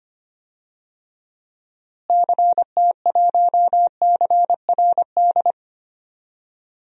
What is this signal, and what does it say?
Morse code, heard as a keyed tone, about 25 wpm, 700 Hz. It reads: CT1CRD
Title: DXCC-100 Call Signs - 6 Characters - Repeated 2x - 25wpm